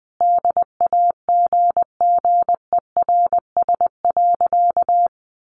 Morse code, heard as a keyed tone, about 19 wpm, 700 Hz. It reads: DAGGERS.
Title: A Sound Of Thunder 20wpm